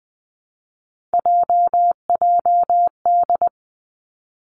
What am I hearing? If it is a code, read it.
Morse code, heard as a keyed tone, about 20 wpm, 700 Hz. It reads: JJD
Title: Three Challenging Letter-Number Combinations 20wpm